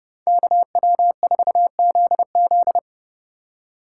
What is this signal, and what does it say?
Morse code, heard as a keyed tone, about 30 wpm, 700 Hz. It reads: KW4ZZ